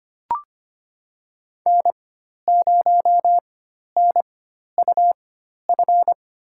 Morse code, heard as a keyed tone, about 25 wpm, spaced as for 12 wpm, 700 Hz. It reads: N0NUF
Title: USA Call Signs - 5 Characters - 4x Character Spacing - 25wpm